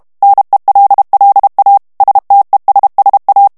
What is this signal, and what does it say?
Morse code, heard as a keyed tone, about 32 wpm, 800 Hz. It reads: NELLA STESSA
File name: texte clair 2 vitesse 30 mots minute